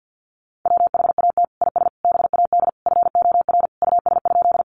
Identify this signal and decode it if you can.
Morse code, heard as a keyed tone, about 50 wpm, 700 Hz. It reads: WHAT IS BAD FOR US?
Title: Sentences from Top 700 Words 50wpm